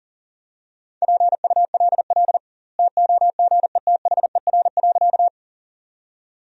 Morse code, heard as a keyed tone, about 37 wpm, 700 Hz. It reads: PULL TOGETHER.
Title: 2x Word-Spacing - Sentences from Top 400 Words - 40wpm